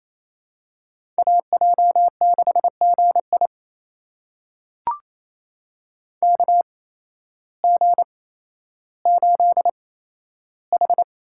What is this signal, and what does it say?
Morse code, heard as a keyed tone, about 28 wpm, spaced as for 13 wpm, 700 Hz. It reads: AJ6GI KG8H
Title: USA Call Signs - 5 Characters - 8x Character Spacing - 28wpm